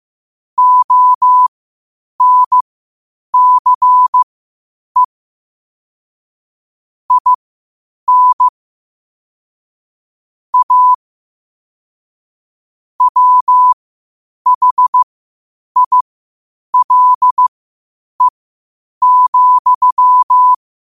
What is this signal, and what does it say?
Morse code, heard as a keyed tone, about 15 wpm, spaced as for 8 wpm, 1000 Hz. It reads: ONCE IN A WHILE,
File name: qotdfiles / 8wpm